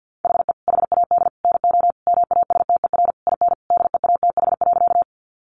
Morse code, heard as a keyed tone, about 50 wpm, 700 Hz. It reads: HE HAD NO MASTER IN DEATH.
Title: Sentences from Top 1300 Words 50wpm